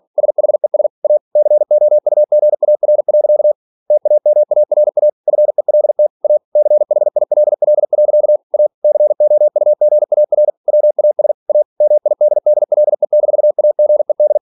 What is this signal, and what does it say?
Morse code, heard as a keyed tone, about 46 wpm, 600 Hz. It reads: SHES A COUGAR. TAMARA FELT A CHILL. A COUGAR WAS A MIDDLE-AGED